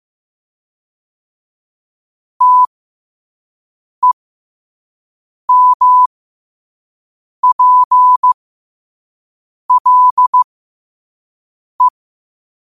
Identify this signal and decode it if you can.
Morse code, heard as a keyed tone, about 15 wpm, spaced as for 5 wpm, 1000 Hz. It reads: TEMPLE
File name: 5wpm